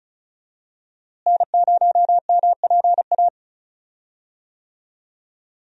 Morse code, heard as a keyed tone, about 34 wpm, 700 Hz. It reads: N0MPA
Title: USA Call Signs - 5 Characters - Repeated 2x - 35wpm